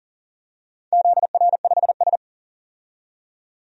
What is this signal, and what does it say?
Morse code, heard as a keyed tone, about 40 wpm, 700 Hz. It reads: ZR5S
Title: DXCC-100 Call Signs - 4 Characters - 40wpm